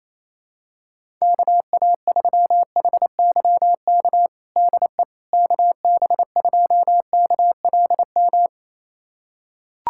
Morse code, heard as a keyed tone, about 27 wpm, 700 Hz. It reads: KA3HYK DE KB2KLM
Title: USA Callsign DE Callsign - 6 Characters - 28wpm